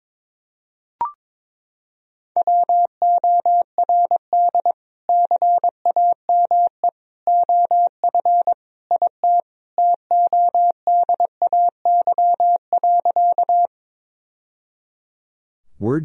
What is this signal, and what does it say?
Morse code, heard as a keyed tone, about 22 wpm, 700 Hz. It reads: WORD CAME OF IT TODAY.
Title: Sentences from Top 1000 Words - Review Part 4 - 22wpm